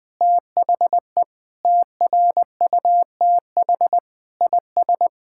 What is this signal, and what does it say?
Morse code, heard as a keyed tone, about 20 wpm, 700 Hz. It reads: THE TRUTH IS
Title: Sentences with 6-Word Phrases - 20wpm